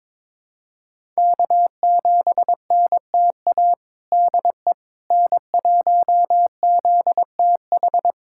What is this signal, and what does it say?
Morse code, heard as a keyed tone, about 22 wpm, 700 Hz. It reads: K7NTA DE N1ZTH
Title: USA Callsign DE Callsign - 5 Characters - 22wpm